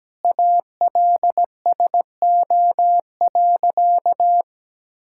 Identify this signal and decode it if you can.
Morse code, heard as a keyed tone, about 17 wpm, 700 Hz. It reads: ALSO.
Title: Sentences from Top 1600 Words 17wpm